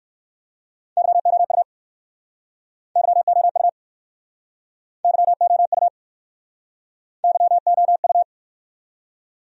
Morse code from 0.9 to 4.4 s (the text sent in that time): YOU YOU